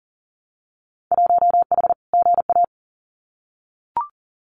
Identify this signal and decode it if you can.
Morse code, heard as a keyed tone, about 39 wpm, 700 Hz. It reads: 1H GA
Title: Winter Field Day Exchanges 40wpm